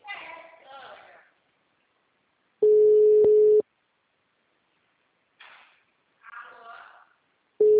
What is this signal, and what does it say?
Ringback tone (Europe, 425 Hz), with speech.